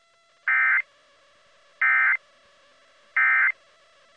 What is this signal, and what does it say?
Emergency Alert System (SAME) tones, heard as harsh buzzing bursts. It is heard from 0.5 to 3.5 s.